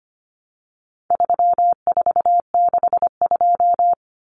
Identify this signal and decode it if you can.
Morse code, heard as a keyed tone, about 25 wpm, 700 Hz. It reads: 3462
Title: Common Cut Numbers - 4-Digit 25wpm